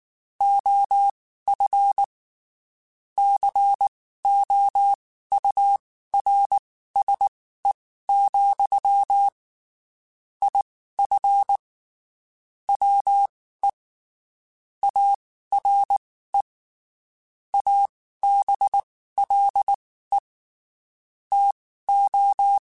Morse code, heard as a keyed tone, about 19 wpm, spaced as for 14 wpm, 800 Hz. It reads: OF COURSE, IF WE ARE ABLE TO
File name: (spaced) / QSOs Spaced-mp3s / WA7QOX 1 @18 sp